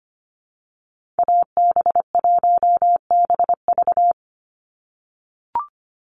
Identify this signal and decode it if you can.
Morse code, heard as a keyed tone, about 25 wpm, 700 Hz. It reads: AB1BV
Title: USA Call Signs 25wpm